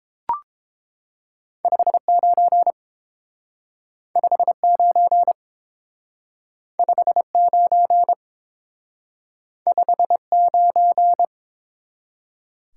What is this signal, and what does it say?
Morse code, heard as a keyed tone, about 28 wpm, 700 Hz. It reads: 59 59 59 59